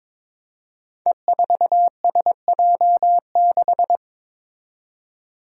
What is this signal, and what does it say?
Morse code, heard as a keyed tone, about 22 wpm, 700 Hz. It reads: E4SJ6
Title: Five Challenging Letter-Number Combinations 22wpm